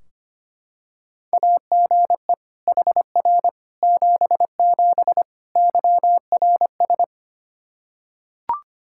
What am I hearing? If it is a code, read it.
Morse code, heard as a keyed tone, about 25 wpm, 700 Hz. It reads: AGE HR 77 YRS
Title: Practice QSO - Exchange 3 - More Info - 25wpm